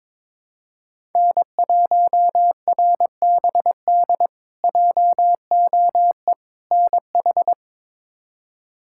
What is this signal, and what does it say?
Morse code, heard as a keyed tone, about 22 wpm, 700 Hz. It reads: N1RBD JOE NH
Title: K1USN SST QSO Practice - 22wpm